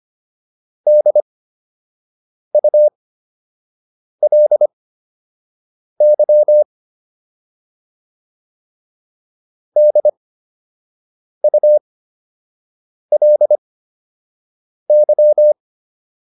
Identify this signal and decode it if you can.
Morse code, heard as a keyed tone, about 25 wpm, spaced as for 6 wpm, 600 Hz. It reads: DULY DULY